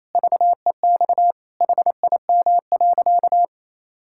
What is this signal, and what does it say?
Morse code, heard as a keyed tone, about 27 wpm, 700 Hz. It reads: VEX HIM.
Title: Pangram Challenge v2 - 28wpm